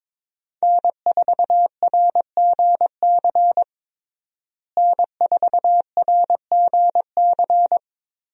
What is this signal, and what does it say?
Morse code, heard as a keyed tone, about 22 wpm, 700 Hz. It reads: N4RGC N4RGC
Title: USA Call Signs - 5 Characters - Repeated 3x - 22wpm